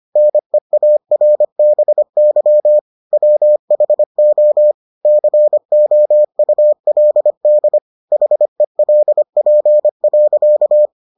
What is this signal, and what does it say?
Morse code, heard as a keyed tone, about 25 wpm, 600 Hz. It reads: NEARBY WHO COULD HELP.